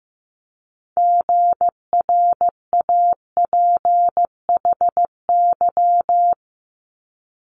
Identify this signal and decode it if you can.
Morse code, heard as a keyed tone, about 15 wpm, 700 Hz. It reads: GRAPHY
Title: Multi-letter Phonemes 15wpm